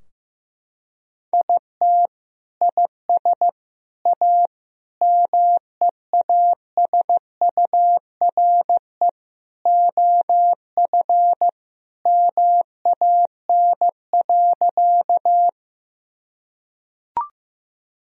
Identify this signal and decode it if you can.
Morse code, heard as a keyed tone, about 15 wpm, 700 Hz. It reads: IT IS A MEASURE OF MAN.
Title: Sentences from Top 1000 Words - Review Part 2 - 15wpm